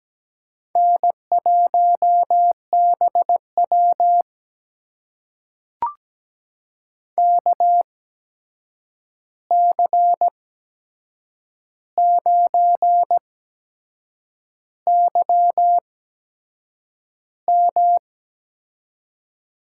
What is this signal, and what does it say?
Morse code, heard as a keyed tone, about 17 wpm, spaced as for 7 wpm, 700 Hz. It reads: N1BW KC9YM
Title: USA Call Signs - 8x Character Spacing - 17wpm